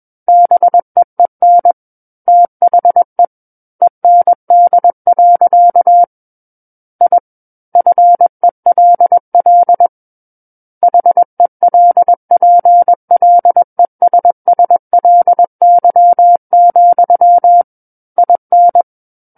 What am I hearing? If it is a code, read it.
Morse code, heard as a keyed tone, about 21 wpm, 700 Hz. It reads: BEEN THE END. I FELL HELPLESSLY, IN
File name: Book One / 12-Chapter 12 - 21 WPM